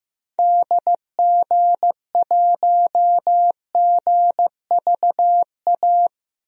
Morse code, heard as a keyed tone, about 15 wpm, 700 Hz. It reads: DG1GVA
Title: DXCC-100 Call Signs - 6 Characters - Repeated 2x - 15wpm